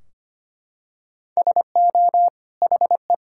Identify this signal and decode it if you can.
Morse code, heard as a keyed tone, about 25 wpm, 700 Hz. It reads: SO HE
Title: Sentences from Top 1600 Words 25wpm